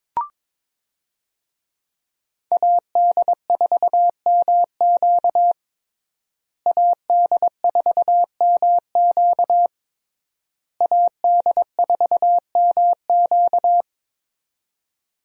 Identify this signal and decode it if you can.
Morse code, heard as a keyed tone, about 22 wpm, 700 Hz. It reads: AD4MQ AD4MQ AD4MQ
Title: USA Call Signs - 5 Characters - Repeated 3x - 22wpm